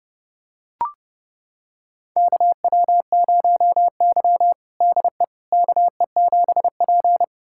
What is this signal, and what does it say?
Morse code, heard as a keyed tone, about 30 wpm, 700 Hz. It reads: KW0Y DE KE7P
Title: USA Callsign DE Callsign - 4 Characters - 30wpm